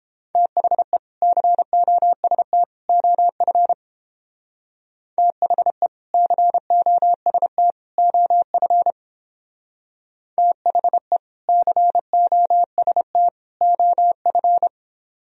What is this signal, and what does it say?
Morse code, heard as a keyed tone, about 29 wpm, 700 Hz. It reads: THE COST OF THE COST OF THE COST OF